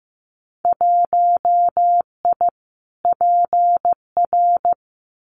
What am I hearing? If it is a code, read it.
Morse code, heard as a keyed tone, about 15 wpm, 700 Hz. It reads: 1I PR
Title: Winter Field Day Exchanges Encore 15wpm